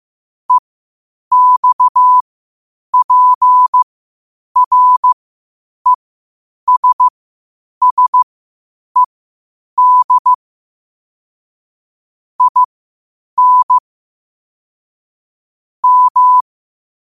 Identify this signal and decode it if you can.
Morse code, heard as a keyed tone, about 15 wpm, spaced as for 8 wpm, 1000 Hz. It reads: EXPRESSED IN M
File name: qotdfiles / 8wpm